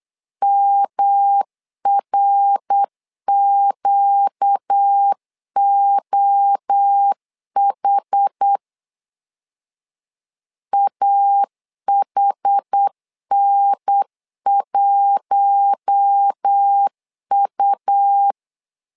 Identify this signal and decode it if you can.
Morse code, heard as a keyed tone, about 8 wpm, 800 Hz. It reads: MRQOH AHN1U
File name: LEZIONI / 4 Q1 8wpm 3035